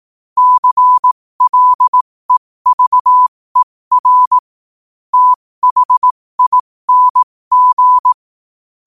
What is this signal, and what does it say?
Morse code, heard as a keyed tone, about 18 wpm, spaced as for 15 wpm, 1000 Hz. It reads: CLEVER THING
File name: qotdfiles / 15wpm